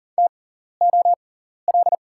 Morse code, heard as a keyed tone, about 41 wpm, spaced as for 14 wpm, 700 Hz. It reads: TOL
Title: 6x Character Spacing - Top 500 Words - 40wpm